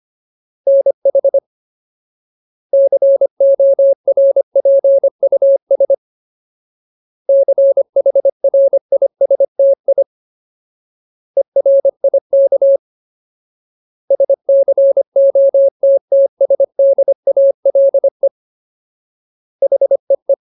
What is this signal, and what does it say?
Morse code, heard as a keyed tone, about 25 wpm, 550 Hz. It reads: NH CORPUS CHRISTI ERIK SCOTTSDALE HEE
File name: Ultimate Word List / 25WPM Ultimate